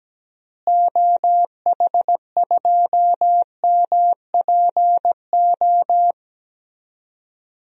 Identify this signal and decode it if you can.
Morse code, heard as a keyed tone, about 17 wpm, 700 Hz. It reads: OH2MPO
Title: DXCC-100 Call Signs - 6 Characters - Repeated 3x - 17wpm